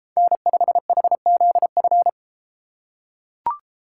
Morse code, heard as a keyed tone, about 32 wpm, 700 Hz. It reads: N5HZF